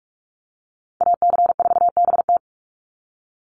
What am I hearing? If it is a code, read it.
Morse code, heard as a keyed tone, about 44 wpm, 700 Hz. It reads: AC4BT
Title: Call Signs - 45wpm